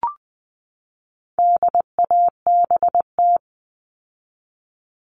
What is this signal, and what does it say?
Morse code, heard as a keyed tone, about 20 wpm, 700 Hz. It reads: DABT
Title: All Cut Numbers - 4-Digit 20wpm